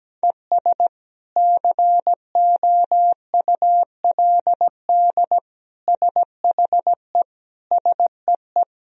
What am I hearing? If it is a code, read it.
Morse code, heard as a keyed tone, about 17 wpm, 700 Hz. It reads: ES COULD SHE SEE
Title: Sentences from Top 100 Words and Abbreviations 17wpm